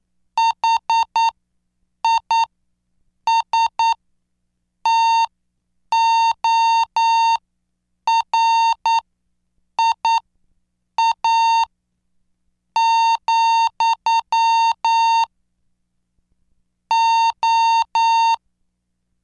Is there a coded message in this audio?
Yes: Morse code from 0.4 to 18.3 s, heard as a keyed tone, about 9 wpm, spaced as for 6 wpm, 910 Hz. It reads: HISTORIA, O